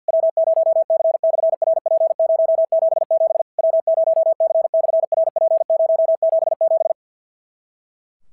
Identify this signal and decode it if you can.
Morse code, heard as a keyed tone, about 54 wpm, 650 Hz. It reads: W0C/RP077 W0C/RP077
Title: SOTA QSO Practice 50wpm